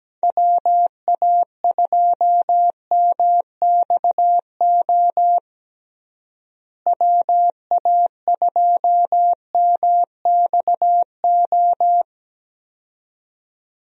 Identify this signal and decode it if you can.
Morse code, heard as a keyed tone, about 17 wpm, 700 Hz. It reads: WA2MXO WA2MXO
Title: USA Call Signs - 6 Characters - Repeated 2x - 17wpm